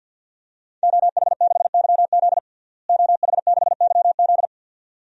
Morse code, heard as a keyed tone, about 51 wpm, 700 Hz. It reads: OH6YZ OH6YZ
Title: DXCC-100 Call Signs - 5 Characters - Repeated 2x - 50wpm